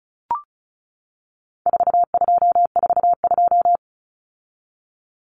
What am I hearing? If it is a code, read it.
Morse code, heard as a keyed tone, about 34 wpm, 700 Hz. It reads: 4242
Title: Numbers - 4-Digit 35wpm